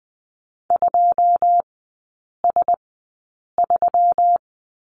Morse code, heard as a keyed tone, about 20 wpm, spaced as for 8 wpm, 690 Hz. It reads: 2S3
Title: 159 Lesson 26e - Letter g - Three Characters 20wpm